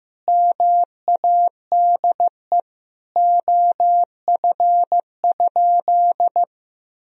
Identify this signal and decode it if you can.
Morse code, heard as a keyed tone, about 16 wpm, 700 Hz. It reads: MADE OF?
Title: Sentences from Top 200 Words and Abbreviations 15wpm